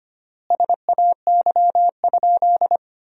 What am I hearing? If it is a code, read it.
Morse code, heard as a keyed tone, about 25 wpm, 700 Hz. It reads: SAY?
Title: 3x Word-Spacing - Sentences from Top 200 Words - 25wpm